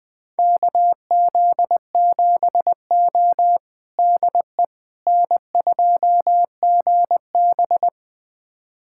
Morse code, heard as a keyed tone, about 20 wpm, 700 Hz. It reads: KZ7O DE N2GB
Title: USA Callsign DE Callsign - 4 Characters - 20wpm